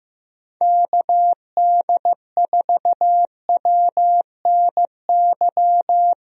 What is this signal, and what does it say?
Morse code, heard as a keyed tone, about 15 wpm, 700 Hz. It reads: KD4WNY
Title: USA Call Signs - 6 Characters - Repeated 2x - 15wpm